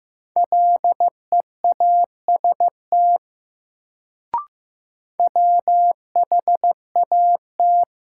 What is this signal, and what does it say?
Morse code, heard as a keyed tone, about 15 wpm, 700 Hz. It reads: LEAST WHAT
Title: Sets of 2 Words - New Word Drill - 15wpm